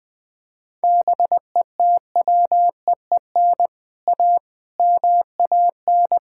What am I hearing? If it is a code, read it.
Morse code, heard as a keyed tone, about 21 wpm, 700 Hz. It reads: BETWEEN A MAN
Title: Sets of 6 Words - 20wpm